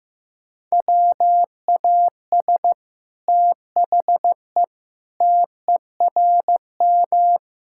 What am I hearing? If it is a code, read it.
Morse code, heard as a keyed tone, about 15 wpm, 700 Hz. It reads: WAS THE TERM
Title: Sentences from Top 1000 Words - Review Part 6 - 15wpm